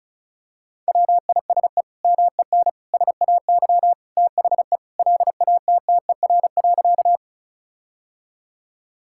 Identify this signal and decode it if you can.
Morse code, heard as a keyed tone, about 36 wpm, 700 Hz. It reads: WISE MEN SAY THE LATTER.
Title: Sentences from Top 1600 Words 35wpm